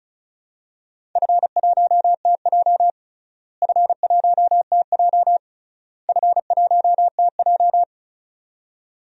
Morse code, heard as a keyed tone, about 34 wpm, 700 Hz. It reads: F1TJ F1TJ F1TJ
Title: DXCC-100 Call Signs - 4 Characters - Repeated 3x - 35wpm